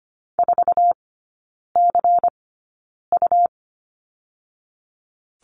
Morse code, heard as a keyed tone, about 25 wpm, 700 Hz. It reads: 4 C U
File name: Lesson 17e - Letter u - Three Characters 25wpm